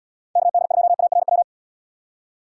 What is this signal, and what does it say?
Morse code, heard as a keyed tone, about 76 wpm, 700 Hz. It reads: KD2ARW